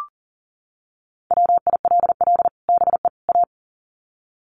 Morse code, heard as a keyed tone, about 40 wpm, 700 Hz. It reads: WILL BE A
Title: Sets of 3 Words 40wpm